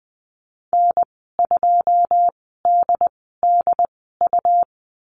Morse code, heard as a keyed tone, about 20 wpm, spaced as for 15 wpm, 700 Hz. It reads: N2DDU
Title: 167 Lesson 27g - Number 7 - Call Signs 20wpm